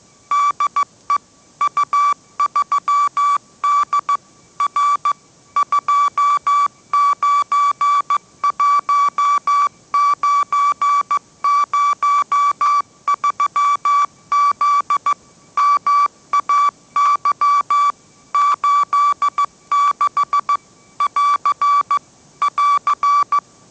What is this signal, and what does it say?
Morse code, heard as a keyed tone, about 15 wpm, 1150 Hz. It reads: DE U3D R 291903Z MAY 86 <AR> <AR>